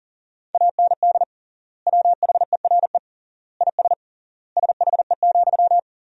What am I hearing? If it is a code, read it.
Morse code, heard as a keyed tone, about 40 wpm, 700 Hz. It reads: AND WHERE IS SHE,
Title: 3x Word-Spacing - Sentences from Top 200 Words - 40wpm